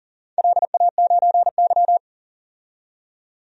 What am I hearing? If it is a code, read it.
Morse code, heard as a keyed tone, about 40 wpm, 700 Hz. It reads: LA9Y
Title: DXCC-100 Call Signs - 4 Characters - 40wpm